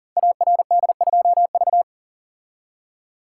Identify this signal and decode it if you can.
Morse code, heard as a keyed tone, about 38 wpm, 690 Hz. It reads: ARD2V
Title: Code Groups - 5 Letter:Numbers - 40wpm